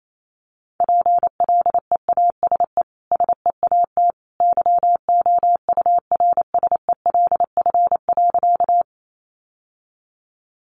Morse code, heard as a keyed tone, about 28 wpm, 700 Hz. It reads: PLEASE SEAT YOURSELF.
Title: Sentences from Top 1500 Words 28wpm